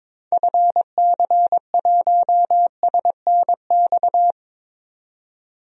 Morse code, heard as a keyed tone, about 21 wpm, 700 Hz. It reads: FC1SNX
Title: DXCC-100 Call Signs - 6 Characters - 22wpm